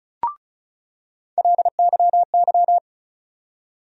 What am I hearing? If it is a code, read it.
Morse code, heard as a keyed tone, about 36 wpm, 700 Hz. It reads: LYY
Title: USA Call Sign Postfixes - 3 Letters 35wpm